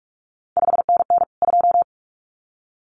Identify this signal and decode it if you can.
Morse code, heard as a keyed tone, about 46 wpm, 700 Hz. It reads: 5NN 2
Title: CQ WW DX Contest Exchange - 45wpm